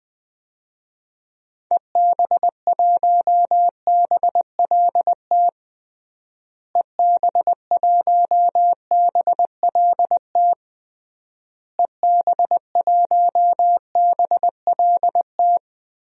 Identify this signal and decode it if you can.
Morse code, heard as a keyed tone, about 21 wpm, 700 Hz. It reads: EB1BLT EB1BLT EB1BLT
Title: DXCC-100 Call Signs - 6 Characters - Repeated 3x - 20wpm